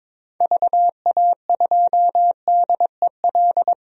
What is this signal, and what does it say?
Morse code, heard as a keyed tone, about 22 wpm, 700 Hz. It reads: VA2DEL
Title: DXCC-100 Call Signs - 6 Characters - 22wpm